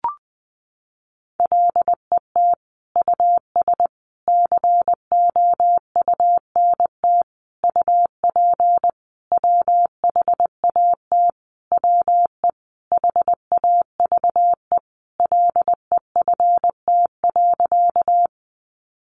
Morse code, heard as a keyed tone, about 20 wpm, 700 Hz. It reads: LET US COUNT UP WHAT WE HAVE LEFT.
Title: Sentences from Top 1500 Words 20wpm